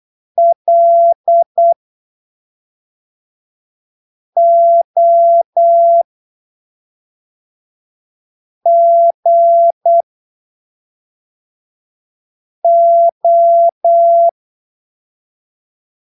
Morse code, heard as a keyed tone, about 8 wpm, spaced as for 3 wpm, 680 Hz. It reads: LOGO
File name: arquivos / texto / texto 08 08ppm